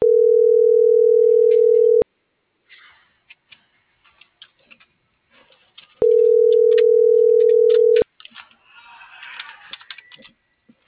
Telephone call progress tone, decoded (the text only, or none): ringback tone